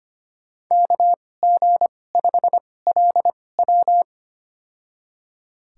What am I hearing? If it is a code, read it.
Morse code, heard as a keyed tone, about 25 wpm, spaced as for 18 wpm, 700 Hz. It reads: KG5LW
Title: USA Call Signs - 5 Characters - 2x Character Spacing - 25wpm